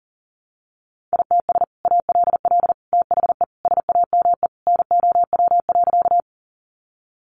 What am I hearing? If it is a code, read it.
Morse code, heard as a keyed tone, about 40 wpm, 700 Hz. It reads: ITS ALL THE SAME NOW.
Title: Sentences from Top 600 Words 40wpm